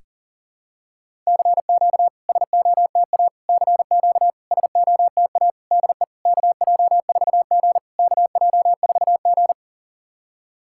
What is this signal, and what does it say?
Morse code, heard as a keyed tone, about 40 wpm, 690 Hz. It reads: CQ SOTA CQ SOTA DE KJ4G KJ4G
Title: SOTA QSO Practice 40wpm